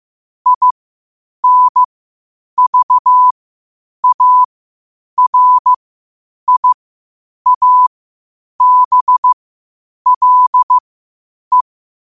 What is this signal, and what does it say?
Morse code, heard as a keyed tone, about 15 wpm, spaced as for 8 wpm, 1000 Hz. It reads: INVARIABLE
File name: qotdfiles / 8wpm